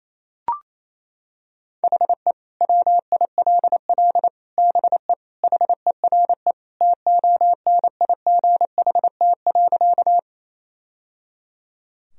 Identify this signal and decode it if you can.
Morse code, heard as a keyed tone, about 28 wpm, 700 Hz. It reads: HE WILL BE HERE TONIGHT.
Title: Sentences from Top 1000 Words - Review Part 3 - 28wpm